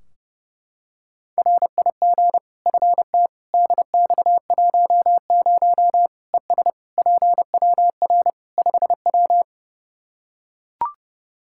Morse code, heard as a keyed tone, about 30 wpm, 700 Hz. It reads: RIG FT DX10 ES PWR 5W
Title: Practice QSO - Exchange 2 - Info - 30wpm